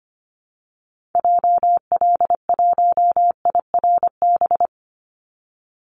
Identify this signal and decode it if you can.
Morse code, heard as a keyed tone, about 25 wpm, 700 Hz. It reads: JL1IRB
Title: Call Signs Repeated 3x - 25wpm